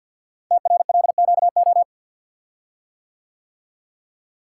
Morse code, heard as a keyed tone, about 50 wpm, 700 Hz. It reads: TRLQO
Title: Code Groups - 5 Letter:Numbers - 50wpm